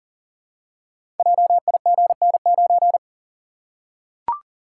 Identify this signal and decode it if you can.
Morse code, heard as a keyed tone, about 39 wpm, 700 Hz. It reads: JIGN9
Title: Code Groups - 5 Letter:Numbers - 40wpm